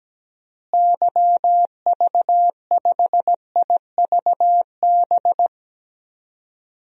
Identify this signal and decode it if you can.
Morse code, heard as a keyed tone, about 17 wpm, 700 Hz. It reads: YV5IVB
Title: DXCC-100 Call Signs - 6 Characters - Repeated 3x - 17wpm